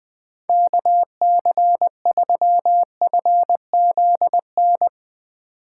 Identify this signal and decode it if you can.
Morse code, heard as a keyed tone, about 20 wpm, 700 Hz. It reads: KC3FZN
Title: Speed Racing - USA 2x3 Call Signs 17wpm